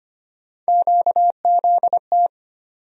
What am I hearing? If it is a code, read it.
Morse code, heard as a keyed tone, about 25 wpm, 700 Hz. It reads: QZT
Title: USA Call Sign Postfixes - 3 Letters 25wpm